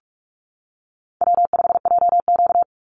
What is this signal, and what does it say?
Morse code, heard as a keyed tone, about 46 wpm, 700 Hz. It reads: W5JQ
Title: Call Signs - 45wpm